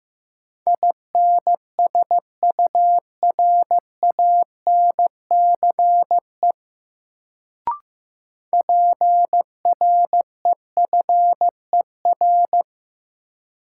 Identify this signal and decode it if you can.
Morse code, heard as a keyed tone, about 15 wpm, 700 Hz. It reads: INSURANCE PREFER
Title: Sentences from Top 1100 Words - New Word Drill - 15wpm